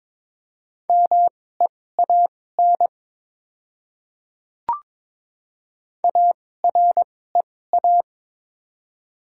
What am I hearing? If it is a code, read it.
Morse code, heard as a keyed tone, about 22 wpm, spaced as for 16 wpm, 700 Hz. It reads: MEAN AREA
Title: 2x Character Spacing - Top 500 Words - 22wpm